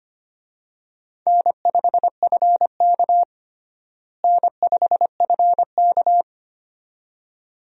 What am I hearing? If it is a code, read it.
Morse code, heard as a keyed tone, about 25 wpm, 700 Hz. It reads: N5FK N5FK
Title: USA Call Signs - 4 Characters - Repeated 2x - 25wpm